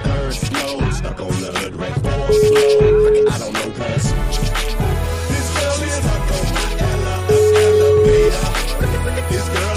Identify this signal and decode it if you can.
Ringback tone (Europe, 425 Hz), with noise.